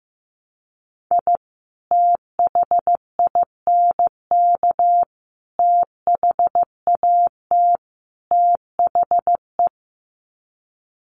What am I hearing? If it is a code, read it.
Morse code, heard as a keyed tone, about 15 wpm, 700 Hz. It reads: I THINK THAT THE
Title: Sets of 4 Words 15wpm